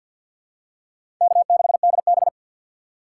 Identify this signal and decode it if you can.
Morse code, heard as a keyed tone, about 47 wpm, 700 Hz. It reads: K6DB
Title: USA Call Signs - 4 Characters - 6x Character Spacing - 50wpm